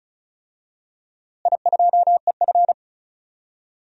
Morse code, heard as a keyed tone, about 36 wpm, 700 Hz. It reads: I2EF
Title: DXCC-100 Call Signs - 4 Characters - Repeated 3x - 35wpm